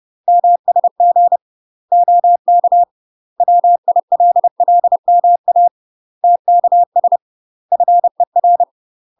Morse code, heard as a keyed tone, about 30 wpm, 700 Hz. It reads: MSG OK WILLMA TKS FER
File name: UK MASTER QSOs TRAINING / 30WPM / UK MASTER QSOs 0013